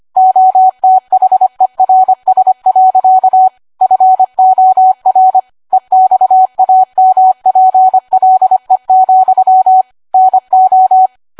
Morse code, heard as a keyed tone, about 25 wpm, 750 Hz. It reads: OTHERS. FOR EXAMPLE, NO